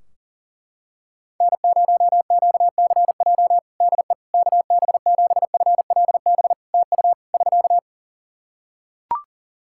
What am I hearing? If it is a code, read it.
Morse code, heard as a keyed tone, about 40 wpm, 700 Hz. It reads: N0QCJ DE KB7FLB TU <SK>
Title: Practice QSO - Exchange 4 - Ending - 40wpm